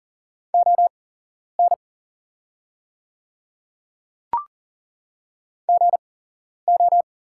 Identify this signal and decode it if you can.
Morse code, heard as a keyed tone, about 40 wpm, spaced as for 11 wpm, 700 Hz. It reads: ON GO
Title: 8x Character Spacing - Top 100 Words - 40wpm